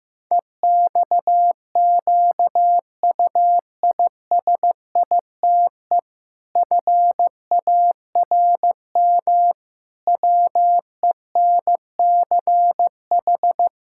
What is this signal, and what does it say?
Morse code, heard as a keyed tone, about 15 wpm, 700 Hz. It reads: EXQUISITE FARM WENCH
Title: Pangram Challenge v2 - 15wpm